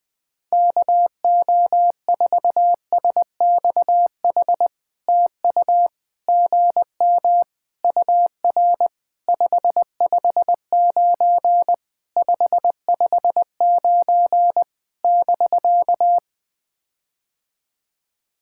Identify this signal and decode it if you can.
Morse code, heard as a keyed tone, about 20 wpm, 700 Hz. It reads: KO4SXH TU GM UR 559 559 <BK>
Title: POTA QSO Practice - 20wpm